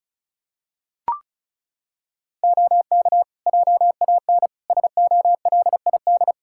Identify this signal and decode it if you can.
Morse code, heard as a keyed tone, about 34 wpm, 700 Hz. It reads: OK JAN SOLID
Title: K1USN SST QSO Practice 35wpm